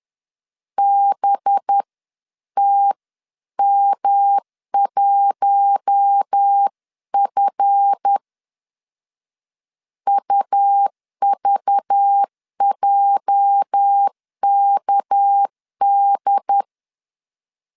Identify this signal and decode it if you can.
Morse code, heard as a keyed tone, about 10 wpm, 800 Hz. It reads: BTM1F UVJKD
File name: LEZIONI / 8 YJ45 / 8 YJ4 10wpm 2633